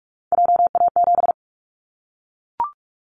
Morse code, heard as a keyed tone, about 44 wpm, 700 Hz. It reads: JA7
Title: Callsign Prefixes for Top 25 DXCC - Repeated 3x - 45wpm